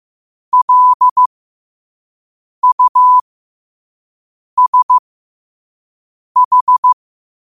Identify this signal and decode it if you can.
Morse code, heard as a keyed tone, about 15 wpm, spaced as for 5 wpm, 1000 Hz. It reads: LUSH
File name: qotdfiles / 5wpm